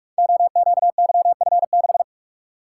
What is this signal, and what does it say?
Morse code, heard as a keyed tone, about 44 wpm, 700 Hz. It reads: OQYF6
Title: Code Groups - 5 Letter:Numbers - 45wpm